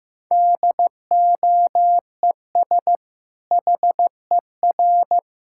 Morse code, heard as a keyed tone, about 16 wpm, 700 Hz. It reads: DOES HER
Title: Sentences from Top 1000 Words - Review Part 6 - 15wpm